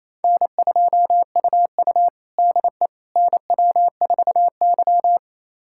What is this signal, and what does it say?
Morse code, heard as a keyed tone, about 29 wpm, 690 Hz. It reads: N2UU DE NW4Y
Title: USA Callsign DE Callsign - 4 Characters - 28wpm